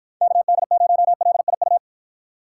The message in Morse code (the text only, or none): KD0LIU